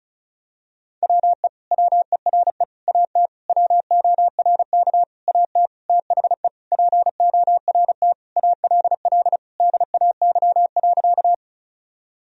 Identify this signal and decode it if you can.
Morse code, heard as a keyed tone, about 36 wpm, 700 Hz. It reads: WE WERE AT WORK AT THE PORT ALL DAY.
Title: Sentences from Top 1000 Words - Review Part 1 - 35wpm